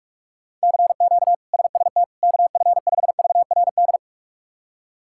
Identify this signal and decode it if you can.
Morse code, heard as a keyed tone, about 41 wpm, 700 Hz. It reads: CQ SST KF5VRD
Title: K1USN SST QSO Practice - 45wpm